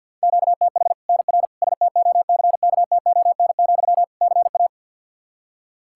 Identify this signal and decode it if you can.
Morse code, heard as a keyed tone, about 52 wpm, 700 Hz. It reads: QTH NR STOCKTON, CA
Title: Practice QSO - Exchange 1 - Beginning - 50wpm